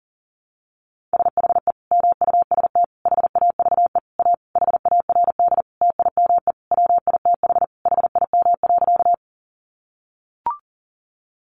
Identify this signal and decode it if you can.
Morse code, heard as a keyed tone, about 40 wpm, 700 Hz. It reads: SHE MUST HAVE A HARD TIME WITH HIM.
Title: Sentences from Top 300 Words 40wpm